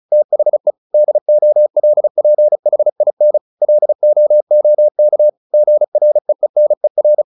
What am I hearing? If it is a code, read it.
Morse code, heard as a keyed tone, about 35 wpm, 600 Hz. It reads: THE DOLPHIN LOOK GREENER
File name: the old man and the sea 2 35wpm